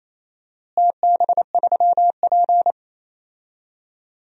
Morse code, heard as a keyed tone, about 27 wpm, 700 Hz. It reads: TB3P
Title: DXCC-100 Call Signs - 4 Characters - Repeated 3x - 28wpm